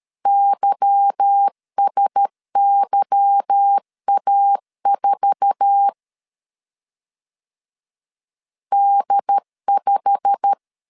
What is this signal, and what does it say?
Morse code, heard as a keyed tone, about 13 wpm, 800 Hz. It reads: YSYA4 D5
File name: CW / DET / 1 9 1500C S B 12wpm 3819